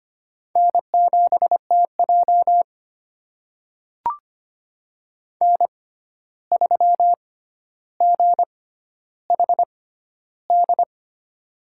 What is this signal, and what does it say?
Morse code, heard as a keyed tone, about 25 wpm, spaced as for 12 wpm, 700 Hz. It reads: N7TJ N3GHD
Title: USA Call Signs - 6x Character Spacing - 25wpm